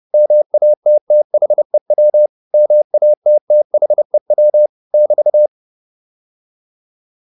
Morse code, heard as a keyed tone, about 30 wpm, 600 Hz. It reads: MATTHEW MATTHEW <BT>